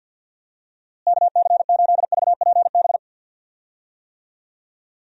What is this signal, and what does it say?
Morse code, heard as a keyed tone, about 46 wpm, 700 Hz. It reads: KC8VPB
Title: USA Call Signs - 6 Characters - Repeated 2x - 50wpm